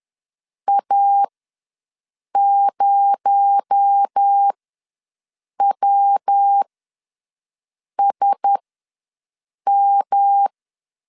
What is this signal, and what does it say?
Morse code, heard as a keyed tone, about 10 wpm, spaced as for 5 wpm, 800 Hz. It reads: A0WSM
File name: LEZIONI / 6 PKX0 / 6 PKX0 ws 10wpm 4220